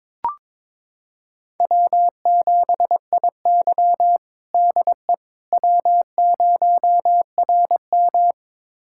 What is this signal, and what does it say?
Morse code, heard as a keyed tone, about 22 wpm, 700 Hz. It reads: W7IY DE W0RM
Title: USA Callsign DE Callsign - 4 Characters - 22wpm